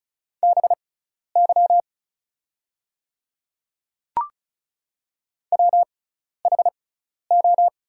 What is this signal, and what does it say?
Morse code, heard as a keyed tone, about 36 wpm, spaced as for 12 wpm, 700 Hz. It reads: BY WHO
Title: 6x Character Spacing - Top 100 Words - 35wpm